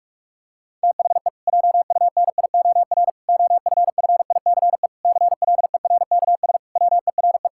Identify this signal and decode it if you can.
Morse code, heard as a keyed tone, about 46 wpm, 700 Hz. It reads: THE JUNIOR OFFICE CLERKS WERE
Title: Pangram Challenge v2 - 45wpm